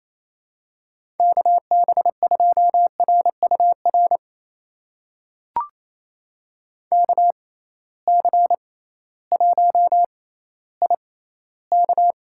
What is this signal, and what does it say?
Morse code, heard as a keyed tone, about 28 wpm, spaced as for 15 wpm, 700 Hz. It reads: KB2RUR KC1IK
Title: USA Call Signs - 6 Characters - 6x Character Spacing - 28wpm